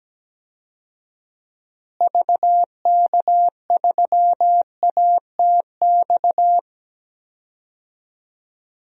Morse code, heard as a keyed tone, about 17 wpm, 700 Hz. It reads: VK3ATX
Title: DXCC-100 Call Signs - 6 Characters - Repeated 3x - 17wpm